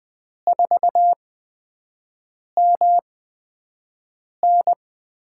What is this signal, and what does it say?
Morse code, heard as a keyed tone, about 20 wpm, 700 Hz. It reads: 4 M N
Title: USA Call Signs - 5 Characters - 8x Character Spacing - 20wpm